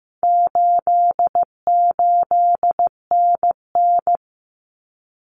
Morse code, heard as a keyed tone, about 15 wpm, 700 Hz. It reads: 88NN
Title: Common Cut Numbers - 4-Digit 15wpm